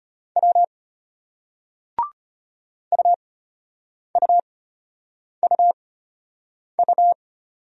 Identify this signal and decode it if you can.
Morse code, heard as a keyed tone, about 34 wpm, spaced as for 8 wpm, 700 Hz. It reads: W UUUU